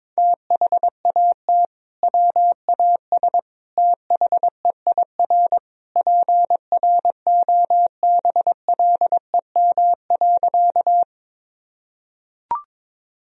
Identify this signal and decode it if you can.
Morse code, heard as a keyed tone, about 21 wpm, 700 Hz. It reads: THAT WAS THEIR PROBLEM.
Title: Sentences from Top 1000 Words - Review Part 2 - 22wpm